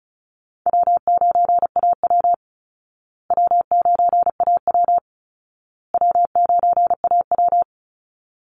Morse code, heard as a keyed tone, about 34 wpm, 700 Hz. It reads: W9AW W9AW W9AW
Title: Easy Call Signs Repeated 3x - 35wpm